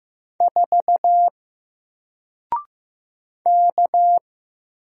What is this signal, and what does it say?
Morse code, heard as a keyed tone, about 15 wpm, 700 Hz. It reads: 4 K
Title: Single Letter-Number Alt 15wpm